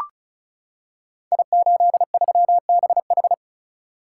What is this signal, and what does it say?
Morse code, heard as a keyed tone, about 36 wpm, 690 Hz. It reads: I83BH
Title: Code Groups - 5 Letter:Numbers - 35wpm